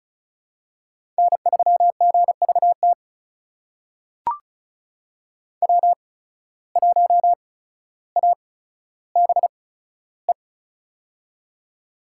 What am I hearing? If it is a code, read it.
Morse code, heard as a keyed tone, about 36 wpm, spaced as for 15 wpm, 700 Hz. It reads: N3GVT W1ABE
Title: USA Call Signs - 5 Characters - 8x Character Spacing - 35wpm